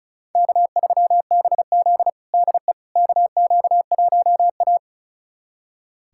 Morse code, heard as a keyed tone, about 36 wpm, 700 Hz. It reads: K3BZ DE KQ1A
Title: USA Callsign DE Callsign - 4 Characters - 35wpm